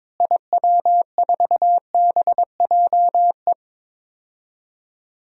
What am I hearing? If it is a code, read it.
Morse code, heard as a keyed tone, about 22 wpm, 700 Hz. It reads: IW4BJE
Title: DXCC-100 Call Signs - 6 Characters - Repeated 2x - 22wpm